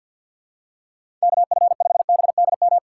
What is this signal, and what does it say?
Morse code, heard as a keyed tone, about 48 wpm, 700 Hz. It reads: KF5BDM
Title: USA Call Signs - 6 Characters - 6x Character Spacing - 50wpm